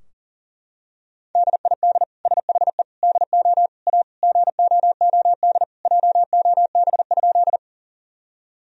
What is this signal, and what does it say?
Morse code, heard as a keyed tone, about 37 wpm, 700 Hz. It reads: DID SHE DO A GOOD JOB?
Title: Sentences from Top 1000 Words - Review Part 3 - 40wpm